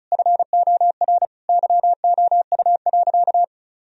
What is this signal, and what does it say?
Morse code, heard as a keyed tone, about 34 wpm, 700 Hz. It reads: FOR YOU.
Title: Sentences from Top 1000 Words - Review Part 1 - 35wpm